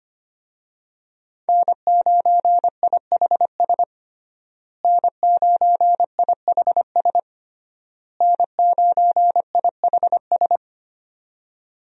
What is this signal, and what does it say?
Morse code, heard as a keyed tone, about 25 wpm, 700 Hz. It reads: N9IHS N9IHS N9IHS
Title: USA Call Signs - 5 Characters - Repeated 3x - 25wpm